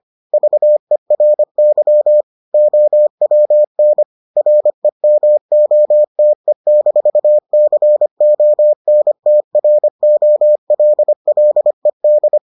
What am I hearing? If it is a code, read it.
Morse code, heard as a keyed tone, about 25 wpm, 600 Hz. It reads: VERY OWN REMOTE-CONTROLLED